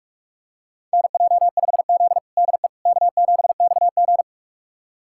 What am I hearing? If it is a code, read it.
Morse code, heard as a keyed tone, about 46 wpm, 690 Hz. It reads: NJ5Z DE K7XG
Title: USA Callsign DE Callsign - 4 Characters - 45wpm